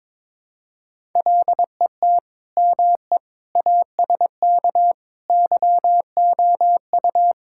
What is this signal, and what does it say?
Morse code, heard as a keyed tone, about 22 wpm, 700 Hz. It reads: LET ME ASK YOU
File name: practice / 22wpm / Build A Phrase - 5-Word Phrases - 22wpm